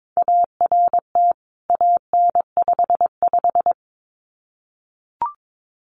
Morse code, heard as a keyed tone, about 22 wpm, 700 Hz. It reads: ART AN55
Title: CWT Exchange 22wpm